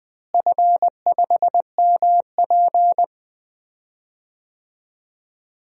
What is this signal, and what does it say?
Morse code, heard as a keyed tone, about 21 wpm, 700 Hz. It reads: F5MP
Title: DXCC-100 Call Signs - 4 Characters - Repeated 2x - 20wpm